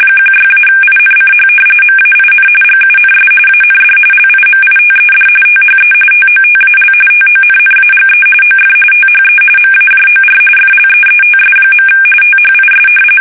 RTTY, about 45.45 baud. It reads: 16-INCH MAIN GUNS AND LIMITED THE TOTAL TONNAGE OF THE MAJOR FLEETS. THIS
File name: fsk850 / battleshipnj